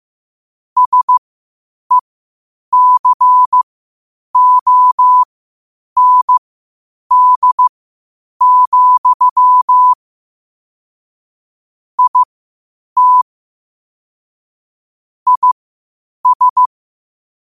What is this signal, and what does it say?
Morse code, heard as a keyed tone, about 15 wpm, spaced as for 8 wpm, 1000 Hz. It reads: SECOND, IT IS
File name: qotdfiles / 8wpm